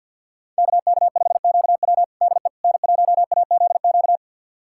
Morse code, heard as a keyed tone, about 53 wpm, 700 Hz. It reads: KK5QW DE N1AZQ
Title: USA Callsign DE Callsign - 5 Characters - 50wpm